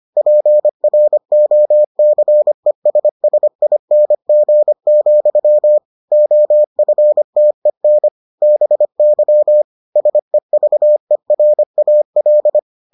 Morse code, heard as a keyed tone, about 25 wpm, 600 Hz. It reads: PROCESSING, OFTEN BY SEVERAL